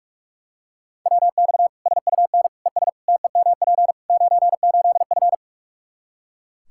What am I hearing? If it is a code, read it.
Morse code, heard as a keyed tone, about 42 wpm, 700 Hz. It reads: WX SUN ES TEMP 98F
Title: K1USN SST QSO Practice 45wpm